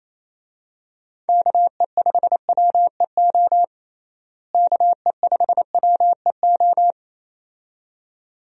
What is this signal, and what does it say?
Morse code, heard as a keyed tone, about 28 wpm, 700 Hz. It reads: KE5WEO KE5WEO
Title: USA Call Signs - 6 Characters - Repeated 2x - 28wpm